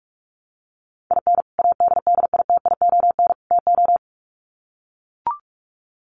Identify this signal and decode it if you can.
Morse code, heard as a keyed tone, about 44 wpm, 700 Hz. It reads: IN ADDITION TO
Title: Sets of 3 Words 45wpm